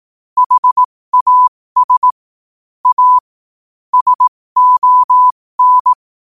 Morse code, heard as a keyed tone, about 18 wpm, spaced as for 15 wpm, 1000 Hz. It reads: HAS A SON
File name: qotdfiles / 15wpm